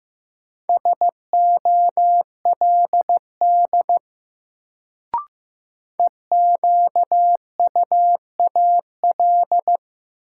Morse code, heard as a keyed tone, about 15 wpm, 700 Hz. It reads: SOLD EQUAL
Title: Sentences from Top 1400 Words - New Word Drill Alt - 15wpm